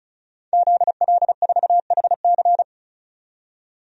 Morse code, heard as a keyed tone, about 35 wpm, 700 Hz. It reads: ZL4HC
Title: DXCC-100 Call Signs - 5 Characters - 35wpm